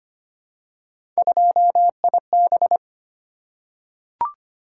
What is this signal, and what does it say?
Morse code, heard as a keyed tone, about 25 wpm, 700 Hz. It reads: 2IB
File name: Three Letter-Number Combinations - 25wpm